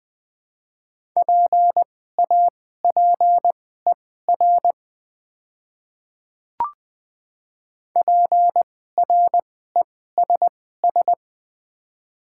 Morse code, heard as a keyed tone, about 21 wpm, spaced as for 15 wpm, 700 Hz. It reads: PAPER PRESS
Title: 2x Character Spacing - Top 300 Words - 20wpm